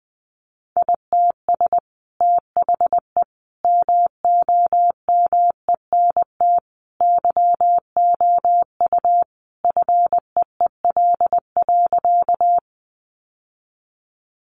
Morse code, heard as a keyed tone, about 20 wpm, 700 Hz. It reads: ITS THE MOMENT YOU FEEL.
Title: Sentences from Top 1000 Words 20wpm